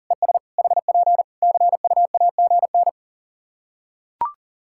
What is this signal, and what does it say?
Morse code, heard as a keyed tone, about 40 wpm, 700 Hz. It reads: ES HP CUAGN
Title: K1USN SST QSO Practice 40wpm